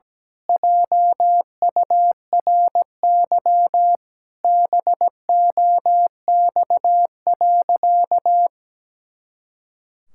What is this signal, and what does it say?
Morse code, heard as a keyed tone, about 17 wpm, 700 Hz. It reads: JURY BOX.
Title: Pangram Challenge v2 - 17wpm